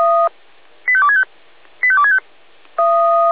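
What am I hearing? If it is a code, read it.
Five-tone selective call (ZVEI), heard as quick runs of five tones, tones of about 70 ms. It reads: 85265 85265
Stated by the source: ZVEI five-tone selective call received on SDR.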